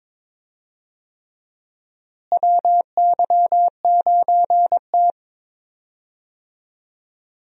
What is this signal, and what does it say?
Morse code, heard as a keyed tone, about 22 wpm, 700 Hz. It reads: WY9T
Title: USA Call Signs - 4 Characters - Repeated 3x - 22wpm